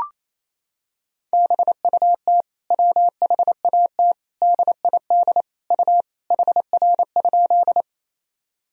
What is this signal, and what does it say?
Morse code, heard as a keyed tone, about 28 wpm, 700 Hz. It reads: BUT WHAT DID U HR?
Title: Sentences from Top 300 Words and Abbreviations 28wpm